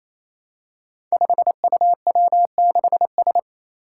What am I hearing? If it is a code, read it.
Morse code, heard as a keyed tone, about 27 wpm, 700 Hz. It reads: 5UW6S
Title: Five Challenging Letter-Number Combinations 28wpm